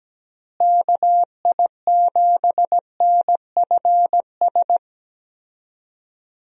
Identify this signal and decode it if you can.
Morse code, heard as a keyed tone, about 17 wpm, 700 Hz. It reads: KI7NFS
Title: USA Call Signs - 6 Characters - Repeated 2x - 17wpm